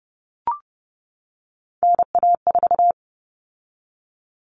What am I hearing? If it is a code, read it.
Morse code, heard as a keyed tone, about 30 wpm, 700 Hz. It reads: NA4
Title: All Cut Numbers - 3-Digit 30wpm